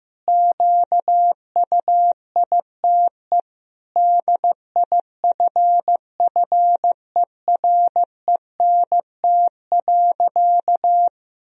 Morse code, heard as a keyed tone, about 15 wpm, 700 Hz. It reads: QUITE DIFFERENT.
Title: Sentences from Top 1000 Words - Review Part 3 - 15wpm